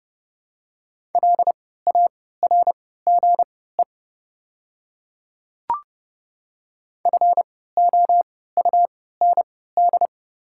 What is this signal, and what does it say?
Morse code, heard as a keyed tone, about 31 wpm, spaced as for 17 wpm, 700 Hz. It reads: LARGE FOUND
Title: 3x Character Spacing - Top 500 Words - 30wpm